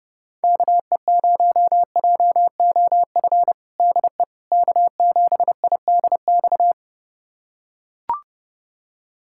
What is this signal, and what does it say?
Morse code, heard as a keyed tone, about 30 wpm, 700 Hz. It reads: KE0JOF DE K7IDX
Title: USA Callsign DE Callsign 30wpm